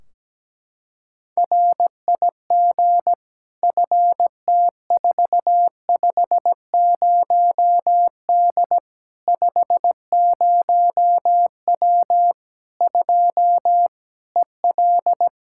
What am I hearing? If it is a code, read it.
Morse code, heard as a keyed tone, about 17 wpm, 700 Hz. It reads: RIG FT450D 50W 2 EL
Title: Practice QSO - Exchange 2 - Info - 17wpm